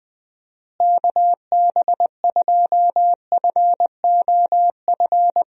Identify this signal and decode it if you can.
Morse code, heard as a keyed tone, about 21 wpm, 700 Hz. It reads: KB2FOF
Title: USA Call Signs - 6 Characters - Repeated 2x - 20wpm